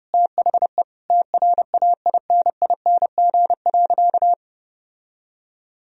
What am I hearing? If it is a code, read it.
Morse code, heard as a keyed tone, about 30 wpm, 700 Hz. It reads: THE TRAINING.
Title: Sentences from Top 1000 Words - Review Part 4 - 30wpm